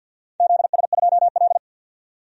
Morse code, heard as a keyed tone, about 47 wpm, 700 Hz. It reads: ZS2L
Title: DXCC-100 Call Signs - 4 Characters - 50wpm